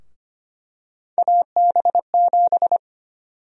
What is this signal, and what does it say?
Morse code, heard as a keyed tone, about 25 wpm, 700 Hz. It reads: AB7
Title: USA Call Sign Prefixes - Repeated 2x - 25wpm